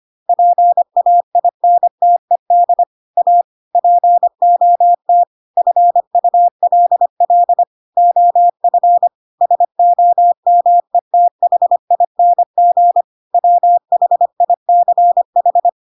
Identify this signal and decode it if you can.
Morse code, heard as a keyed tone, about 25 wpm, 700 Hz. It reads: PAINTED A POT FULL OF SOMETHING WHICH
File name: Pinocchio / Pinocchio ch3 25 25 ik1hge